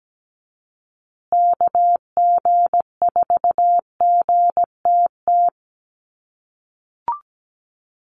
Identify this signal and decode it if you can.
Morse code, heard as a keyed tone, about 17 wpm, 700 Hz. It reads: KG4GTT
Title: USA Call Signs Repeated 2x 17wpm